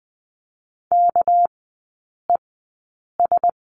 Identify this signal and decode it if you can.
Morse code, heard as a keyed tone, about 20 wpm, spaced as for 8 wpm, 690 Hz. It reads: KES
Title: 198 Lesson 32e - Letter k - Three Characters 20wpm